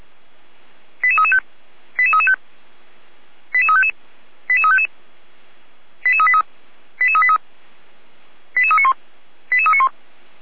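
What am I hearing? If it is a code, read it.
Five-tone selective call (ZVEI), heard as quick runs of five tones, tones of about 80 ms. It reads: 88335 88335 88355 88355 88383 88383 88391 88391
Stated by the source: ZVEI five-tone call received by SDR.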